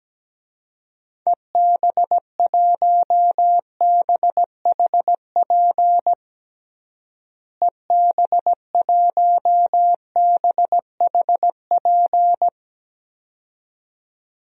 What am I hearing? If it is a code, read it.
Morse code, heard as a keyed tone, about 17 wpm, 700 Hz. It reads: EB1BHP EB1BHP
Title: DXCC-100 Call Signs - 6 Characters - Repeated 2x - 17wpm